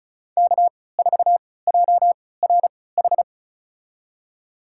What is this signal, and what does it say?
Morse code, heard as a keyed tone, about 36 wpm, spaced as for 20 wpm, 700 Hz. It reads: K4JRH
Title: USA Call Signs - 5 Characters - 3x Character Spacing - 35wpm